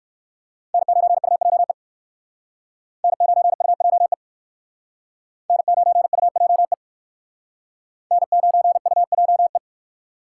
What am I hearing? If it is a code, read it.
Morse code, heard as a keyed tone, about 57 wpm, 700 Hz. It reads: N9UJE N9UJE N9UJE N9UJE